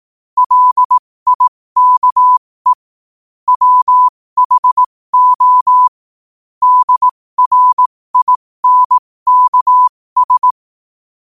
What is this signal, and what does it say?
Morse code, heard as a keyed tone, about 18 wpm, spaced as for 15 wpm, 1000 Hz. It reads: LIKE WHO DRINKS
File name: qotdfiles / 15wpm